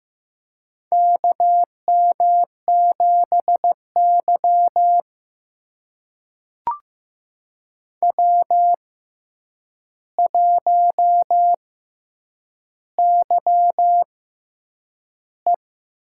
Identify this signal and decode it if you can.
Morse code, heard as a keyed tone, about 15 wpm, spaced as for 8 wpm, 700 Hz. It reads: KM7Y W1YE
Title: USA Call Signs - 4 Characters - 6x Character Spacing - 15wpm